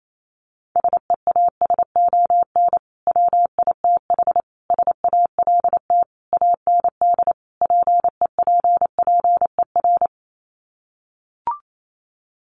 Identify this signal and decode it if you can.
Morse code, heard as a keyed tone, about 28 wpm, 700 Hz. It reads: SEASON WITH SALT AND PEPPER
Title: Sets of 5 Words 28wpm